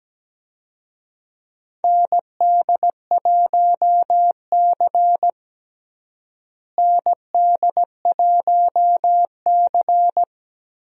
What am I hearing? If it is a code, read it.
Morse code, heard as a keyed tone, about 17 wpm, 700 Hz. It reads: ND1C ND1C
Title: USA Call Signs - 4 Characters - Repeated 3x - 17wpm